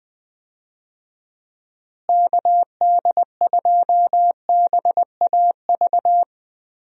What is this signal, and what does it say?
Morse code, heard as a keyed tone, about 20 wpm, 700 Hz. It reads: KD2BAV
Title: USA Call Signs - 6 Characters - Repeated 3x - 20wpm